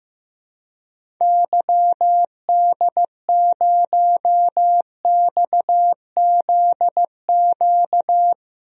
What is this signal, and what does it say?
Morse code, heard as a keyed tone, about 15 wpm, 700 Hz. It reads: YD0XZQ
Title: DXCC-100 Call Signs - 6 Characters - 15wpm